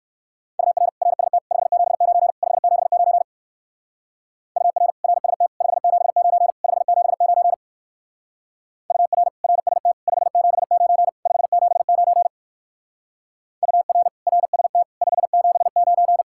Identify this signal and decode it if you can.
Morse code, heard as a keyed tone, about 59 wpm, 700 Hz. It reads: UR RST 579 579 UR RST 579 579 UR RST 579 579 UR RST 579